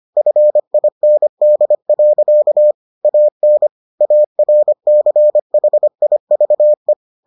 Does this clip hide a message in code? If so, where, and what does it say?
Yes: Morse code from 0.2 to 6.9 s, heard as a keyed tone, about 25 wpm, 600 Hz. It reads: FIND. AN ARCHIVE